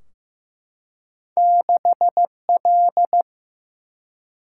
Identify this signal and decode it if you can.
Morse code, heard as a keyed tone, about 15 wpm, 700 Hz. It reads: 6 L
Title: Single-Letter Single-Number 15wpm